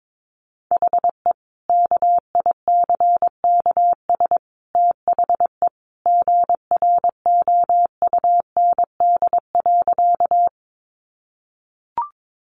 Morse code, heard as a keyed tone, about 22 wpm, 700 Hz. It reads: HE KICKS THE GROUND.
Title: Additional Easy Phrases 22wpm